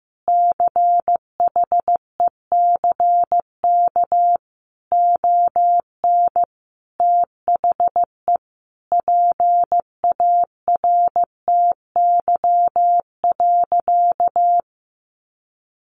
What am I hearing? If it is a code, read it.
Morse code, heard as a keyed tone, about 15 wpm, 700 Hz. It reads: CHECK ON THE PARTY.
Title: Sentences from Top 1400 Words 15wpm